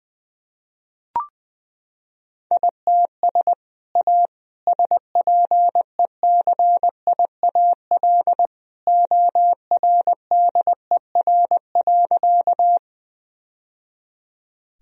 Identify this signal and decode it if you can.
Morse code, heard as a keyed tone, about 20 wpm, 700 Hz. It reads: ITS A SPECIAL ORDER.
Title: Sentences from Top 1000 Words - Review Part 3 - 20wpm